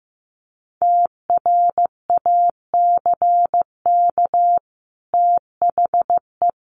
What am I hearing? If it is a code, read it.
Morse code, heard as a keyed tone, about 15 wpm, 700 Hz. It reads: TRACK THE
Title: Sentences from Top 1100 Words 15wpm